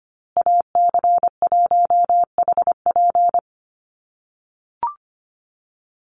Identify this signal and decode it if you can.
Morse code, heard as a keyed tone, about 25 wpm, 700 Hz. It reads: AC1HP
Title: USA Call Signs Repeated 2x 25wpm